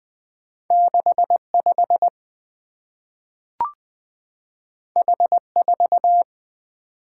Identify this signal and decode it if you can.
Morse code, heard as a keyed tone, about 20 wpm, 700 Hz. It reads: 6 5 H 4
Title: Two Letter-Number Combinations - 20wpm